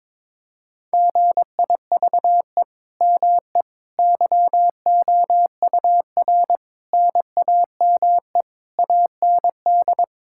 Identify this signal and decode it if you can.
Morse code, heard as a keyed tone, about 22 wpm, 700 Hz. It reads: GIVE ME YOUR NAME AND
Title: Sentences from Top 1000 Words - Review Part 6 - 22wpm